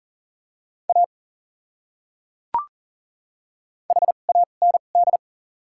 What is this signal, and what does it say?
Morse code, heard as a keyed tone, about 37 wpm, spaced as for 9 wpm, 700 Hz. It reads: AHAND
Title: 2x Character Spacing - Top 500 Words - 40wpm